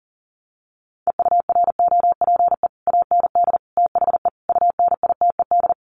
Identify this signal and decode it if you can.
Morse code, heard as a keyed tone, about 40 wpm, 700 Hz. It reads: EUROPE AND THE UNITED
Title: Sets of 5 Words Encore2 40wpm